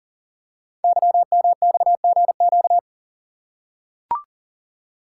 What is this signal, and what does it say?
Morse code, heard as a keyed tone, about 42 wpm, 700 Hz. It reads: YMXGQ
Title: Code Groups - 5 Letters - 40wpm